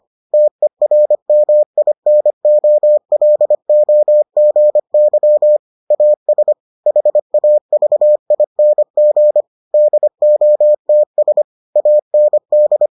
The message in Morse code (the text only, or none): TERMINOLOGY AS HAVING DOTS AND